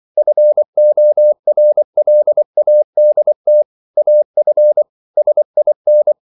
Morse code, heard as a keyed tone, about 24 wpm, 600 Hz. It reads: FORLADT AF SIN
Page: Ferie 24 wpm